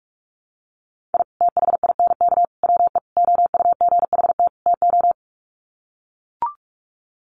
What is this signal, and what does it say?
Morse code, heard as a keyed tone, about 44 wpm, 700 Hz. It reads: I THINK WE OUGHT TO
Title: Sets of 5 Words Encore 45wpm